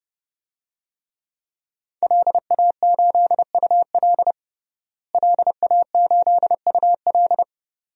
Morse code, heard as a keyed tone, about 32 wpm, 700 Hz. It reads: LA8UL LA8UL
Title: DXCC-100 Call Signs - 5 Characters - Repeated 2x - 30wpm